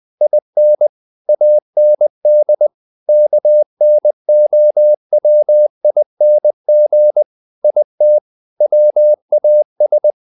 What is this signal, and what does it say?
Morse code, heard as a keyed tone, about 20 wpm, 600 Hz. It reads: IN AND KNOWING IT WAS
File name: the old man and the sea part 2 20wpm